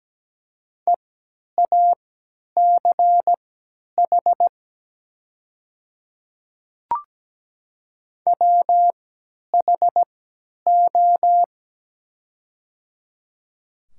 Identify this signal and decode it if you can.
Morse code, heard as a keyed tone, about 17 wpm, spaced as for 10 wpm, 700 Hz. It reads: EACH WHO
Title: 3x Character Spacing - Top 100 Words - 17wpm